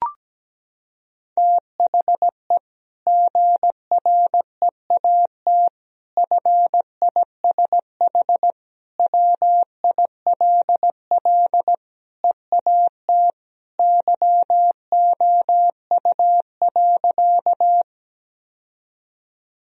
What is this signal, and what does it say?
Morse code, heard as a keyed tone, about 17 wpm, 700 Hz. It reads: THE GREAT FISH WILL EAT YOU.
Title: Sentences from Top 1000 Words - Review Part 1 - 17wpm